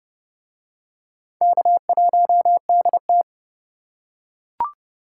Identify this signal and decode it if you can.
Morse code, heard as a keyed tone, about 27 wpm, 700 Hz. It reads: K1DT